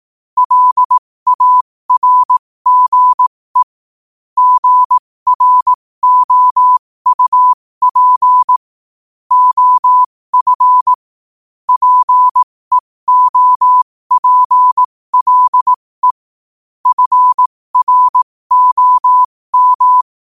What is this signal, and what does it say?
Morse code, heard as a keyed tone, about 18 wpm, spaced as for 15 wpm, 1000 Hz. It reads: LARGE GROUP OF PEOPLE FROM
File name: qotdfiles / 15wpm